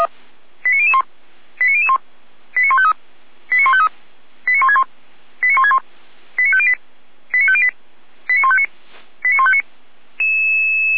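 Five-tone selective call (ZVEI), heard as quick runs of five tones, tones of about 70 ms. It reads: 89001 89001 89153 89153 89161 89161 89508 89508 89159 89159
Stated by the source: ZVEI five-tone selective call received with an SDR, from the Feuerwehr Orlamünde fire brigade audio server.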